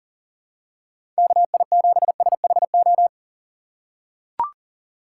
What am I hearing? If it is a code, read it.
Morse code, heard as a keyed tone, about 38 wpm, 700 Hz. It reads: KI7SHO
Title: USA Call Signs - 6 Characters - 3x Character Spacing - 40wpm